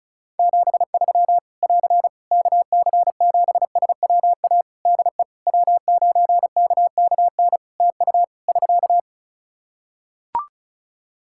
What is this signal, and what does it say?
Morse code, heard as a keyed tone, about 36 wpm, 700 Hz. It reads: 73 <AR> KC7SWA DE W9KKN TU <SK>
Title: Practice QSO - Exchange 4 - Ending - 35wpm